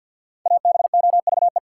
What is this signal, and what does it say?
Morse code, heard as a keyed tone, about 52 wpm, 700 Hz. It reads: ABOVE
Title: Sentences from Top 1200 Words - New Word Drill Alt - 50wpm